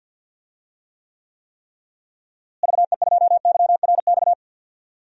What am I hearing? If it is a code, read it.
Morse code, heard as a keyed tone, about 51 wpm, 700 Hz. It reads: VE2YRX
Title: DXCC-100 Call Signs - 6 Characters - Repeated 3x - 50wpm